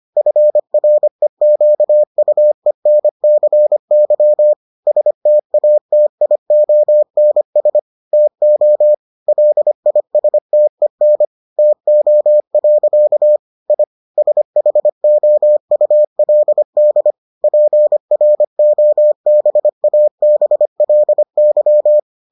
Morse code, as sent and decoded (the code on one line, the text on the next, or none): ..-. .-. --.- ..- . -. -.-. -.-- / ... - .- - .. --- -. ... / - --- / .-.. .. ... - . -. / - --- .-.-.- / .. / ... .... --- ..- .-.. -.. / .--. .-. --- -... .- -... .-.. -.--
FREQUENCY STATIONS TO LISTEN TO. I SHOULD PROBABLY